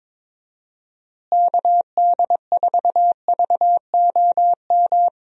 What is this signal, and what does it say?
Morse code, heard as a keyed tone, about 22 wpm, 700 Hz. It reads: KD4VOM
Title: USA Call Signs - 6 Characters - Repeated 3x - 22wpm